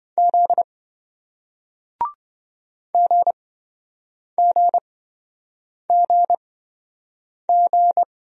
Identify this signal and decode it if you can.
Morse code, heard as a keyed tone, about 27 wpm, spaced as for 7 wpm, 700 Hz. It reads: Z GGGG